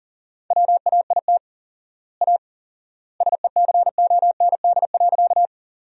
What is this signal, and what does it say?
Morse code, heard as a keyed tone, about 40 wpm, 700 Hz. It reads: WAIT A SECOND.
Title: 4x Word-Spacing - Sentences from Top 500 Words - 40wpm